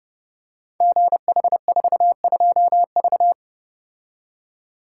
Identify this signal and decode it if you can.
Morse code, heard as a keyed tone, about 30 wpm, 700 Hz. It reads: GH42V
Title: Code Groups - 5 Letter:Numbers - 30wpm